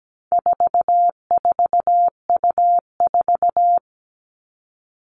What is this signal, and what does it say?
Morse code, heard as a keyed tone, about 17 wpm, 700 Hz. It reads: 44U4
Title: All Cut Numbers - 4-Digit 17wpm